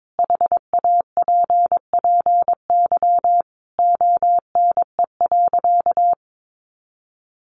Morse code, heard as a keyed tone, about 22 wpm, 700 Hz. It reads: HAPPY ONE.
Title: Sentences from Top 1400 Words 22wpm